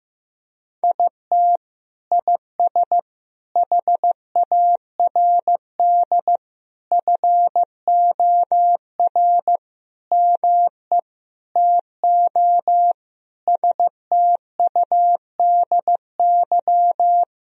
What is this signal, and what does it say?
Morse code, heard as a keyed tone, about 15 wpm, 700 Hz. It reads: IT IS HARD FOR ME TO STUDY
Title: Sentences from Top 1000 Words - Review Part 3 - 15wpm